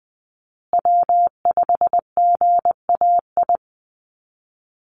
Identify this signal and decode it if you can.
Morse code, heard as a keyed tone, about 20 wpm, 700 Hz. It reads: W5GAI
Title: Call Signs Repeated 2x - 20wpm